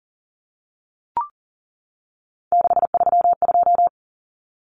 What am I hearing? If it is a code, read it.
Morse code, heard as a keyed tone, about 40 wpm, 700 Hz. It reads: 632
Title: Common Cut Numbers - 3-Digit 40wpm